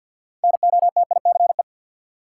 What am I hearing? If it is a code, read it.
Morse code, heard as a keyed tone, about 53 wpm, 700 Hz. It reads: NOTICE
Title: Six Letter Words - 50wpm